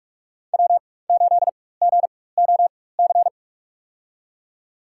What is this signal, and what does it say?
Morse code, heard as a keyed tone, about 45 wpm, spaced as for 21 wpm, 700 Hz. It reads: W8GOC
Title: USA Call Signs - 5 Characters - 4x Character Spacing - 45wpm